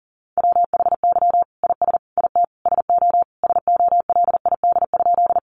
Morse code, heard as a keyed tone, about 40 wpm, 700 Hz. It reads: WHY IS IT SO SOLID?
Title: Sentences from Top 1500 Words 40wpm